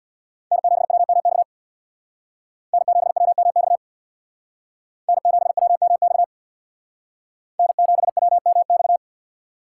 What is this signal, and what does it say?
Morse code, heard as a keyed tone, about 65 wpm, 700 Hz. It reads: N7WMX N7WMX N7WMX N7WMX